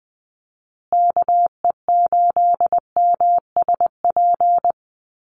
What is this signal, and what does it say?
Morse code, heard as a keyed tone, about 20 wpm, 700 Hz. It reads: KE8MSP
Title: USA Call Signs Repeated 2x 20wpm